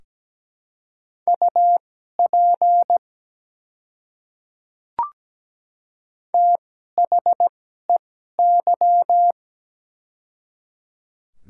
Morse code, heard as a keyed tone, about 17 wpm, spaced as for 12 wpm, 690 Hz. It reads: UP THEY
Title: 2x Character Spacing - Top 100 Words - 17wpm